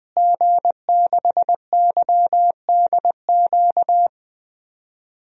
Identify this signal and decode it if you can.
Morse code, heard as a keyed tone, about 20 wpm, 700 Hz. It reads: G6YDQ
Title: DXCC-100 Call Signs - 5 Characters - Repeated 2x - 20wpm